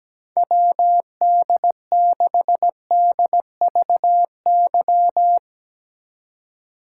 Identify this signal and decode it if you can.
Morse code, heard as a keyed tone, about 17 wpm, 700 Hz. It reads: WD6DVY
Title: Speed Racing - USA 2x3 Call Signs 17wpm